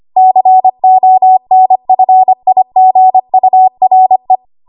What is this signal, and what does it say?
Morse code, heard as a keyed tone, about 25 wpm, 750 Hz. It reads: CONFIGURE
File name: Archive / 25 WPM / 25WPM